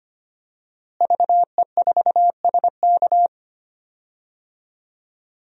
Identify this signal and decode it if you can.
Morse code, heard as a keyed tone, about 25 wpm, 700 Hz. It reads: VE4SK
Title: DXCC-100 Call Signs - 5 Characters - Repeated 2x - 25wpm